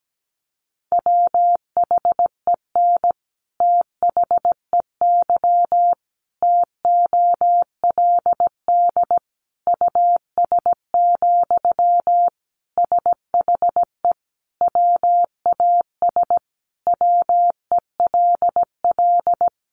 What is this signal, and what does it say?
Morse code, heard as a keyed tone, about 17 wpm, 700 Hz. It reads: WHEN THEY TOLD US, SHE WAS WELL
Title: Sentences from Top 400 Words 17wpm